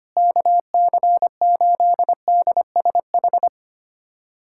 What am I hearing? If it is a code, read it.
Morse code, heard as a keyed tone, about 25 wpm, 700 Hz. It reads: KC8DSH
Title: USA Callsign DE Callsign 25wpm